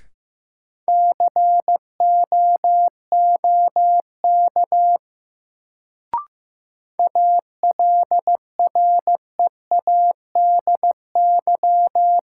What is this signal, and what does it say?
Morse code, heard as a keyed tone, about 15 wpm, 700 Hz. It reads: COOK ALREADY
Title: Sentences from Top 700 Words - New Word Drill - 15wpm